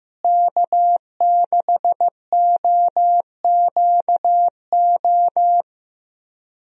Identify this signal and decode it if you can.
Morse code, heard as a keyed tone, about 15 wpm, 700 Hz. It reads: K6OQO
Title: USA Call Signs - 5 Characters - Repeated 3x - 15wpm